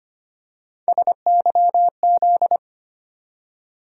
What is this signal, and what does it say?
Morse code, heard as a keyed tone, about 25 wpm, 700 Hz. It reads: SYZ
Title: USA Call Sign Postfixes - 3 Letters 25wpm